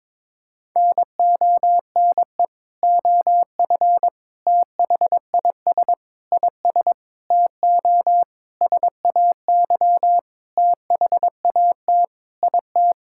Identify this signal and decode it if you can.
Morse code, heard as a keyed tone, about 22 wpm, 700 Hz. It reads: NONE OF THIS IS TO SAY THAT IT
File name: Sentences with 7-Word Phrases - 22wpm